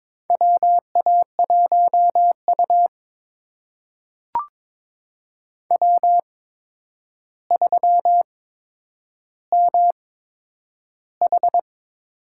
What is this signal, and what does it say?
Morse code, heard as a keyed tone, about 21 wpm, spaced as for 9 wpm, 700 Hz. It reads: WA1U W3MH
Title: USA Call Signs - 8x Character Spacing - 22wpm